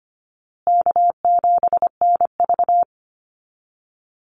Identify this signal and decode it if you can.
Morse code, heard as a keyed tone, about 25 wpm, 700 Hz. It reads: K7NV
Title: Easy Call Signs Repeated 3x - 25wpm